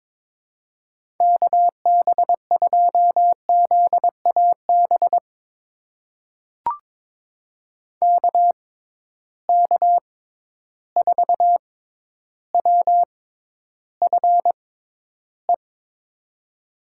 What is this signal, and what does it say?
Morse code, heard as a keyed tone, about 22 wpm, spaced as for 11 wpm, 700 Hz. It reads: KB2ZAB KK4WFE
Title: USA Call Signs - 6 Characters - 6x Character Spacing - 22wpm